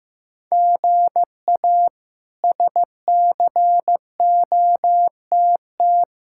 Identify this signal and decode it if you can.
Morse code, heard as a keyed tone, about 15 wpm, 700 Hz. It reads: GA SCOTT
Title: K1USN SST QSO Practice - 15wpm